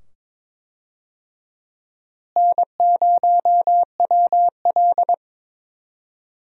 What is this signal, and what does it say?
Morse code, heard as a keyed tone, about 22 wpm, 700 Hz. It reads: N0WL
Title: USA Call Signs - 4 Characters - Repeated 2x - 22wpm